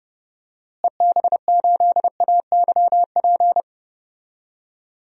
Morse code, heard as a keyed tone, about 32 wpm, 700 Hz. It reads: EB8AYP
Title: DXCC-100 Call Signs - 6 Characters - Repeated 3x - 30wpm